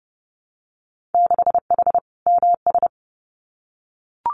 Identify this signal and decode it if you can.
Morse code, heard as a keyed tone, about 30 wpm, 700 Hz. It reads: 6H MS
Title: Winter Field Day Exchanges Encore 30wpm